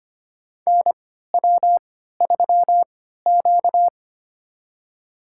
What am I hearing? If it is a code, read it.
Morse code, heard as a keyed tone, about 25 wpm, spaced as for 14 wpm, 700 Hz. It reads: NW3Q
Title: USA Call Signs - 4 Characters - 3x Character Spacing - 25wpm